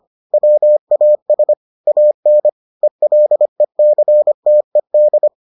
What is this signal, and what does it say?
Morse code, heard as a keyed tone, about 25 wpm, 600 Hz. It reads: WAS AN ELECTED